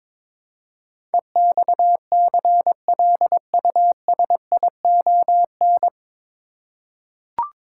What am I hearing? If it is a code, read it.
Morse code, heard as a keyed tone, about 22 wpm, 700 Hz. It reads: EXCLUSION
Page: Morse Code Ninja practice 22wpm